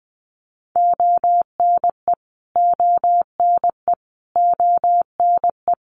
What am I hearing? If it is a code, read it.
Morse code, heard as a keyed tone, about 20 wpm, 700 Hz. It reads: ONE ONE ONE
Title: Top 500 Words Repeated 20wpm